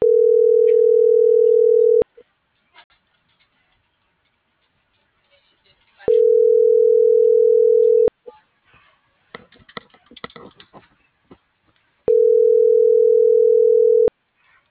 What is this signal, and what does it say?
Ringback tone (North America), 440 + 480 Hz.